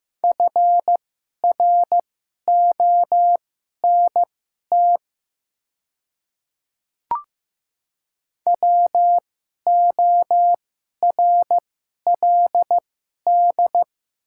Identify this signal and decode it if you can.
Morse code, heard as a keyed tone, about 15 wpm, spaced as for 11 wpm, 700 Hz. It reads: FRONT WORLD
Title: 2x Character Spacing - Top 500 Words - 15wpm